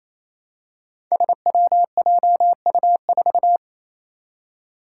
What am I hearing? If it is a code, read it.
Morse code, heard as a keyed tone, about 29 wpm, 700 Hz. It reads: SWJU4
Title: Five Challenging Letter-Number Combinations 28wpm